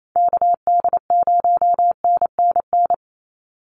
Morse code, heard as a keyed tone, about 28 wpm, 700 Hz. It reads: KD0NNN
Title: USA Call Signs 28wpm